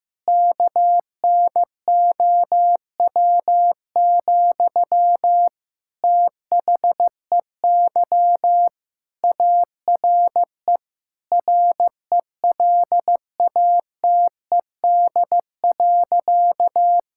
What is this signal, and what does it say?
Morse code, heard as a keyed tone, about 15 wpm, 700 Hz. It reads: KNOW, THEY ARE RELATED.
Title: Sentences from Top 1000 Words - Review Part 5 - 15wpm